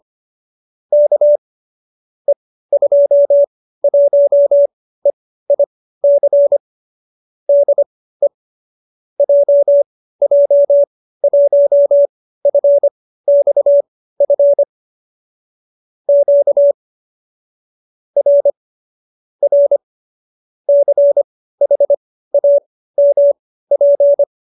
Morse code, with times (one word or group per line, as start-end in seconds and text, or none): 0.9-1.4 K
2.3-6.6 E21EIC
7.5-8.3 DE
9.2-14.6 JJ1FXF
16.1-16.7 Q
18.2-18.5 R
19.4-19.8 R
20.7-24.2 CHAMP